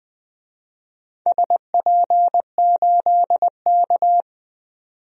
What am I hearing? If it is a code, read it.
Morse code, heard as a keyed tone, about 21 wpm, 700 Hz. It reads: SP8K
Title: DXCC-100 Call Signs - 4 Characters - 20wpm